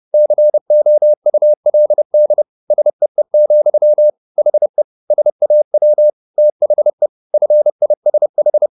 Morse code, heard as a keyed tone, about 30 wpm, 600 Hz. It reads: COULD SEE, HE SAW THE FISH